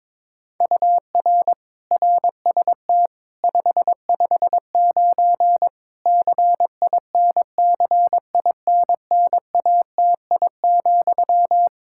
Morse code, heard as a keyed tone, about 22 wpm, 700 Hz. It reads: UR RST 559 CINCINNATI,
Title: Practice QSO - Exchange 1 - Beginning - 22wpm